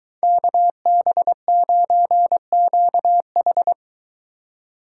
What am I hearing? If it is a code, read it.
Morse code, heard as a keyed tone, about 23 wpm, 700 Hz. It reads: KB9QH